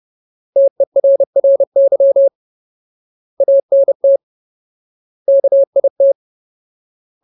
Morse code, heard as a keyed tone, about 30 wpm, 550 Hz. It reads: TERRY ANT KIT
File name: Ultimate Word List / 30WPM Ultimate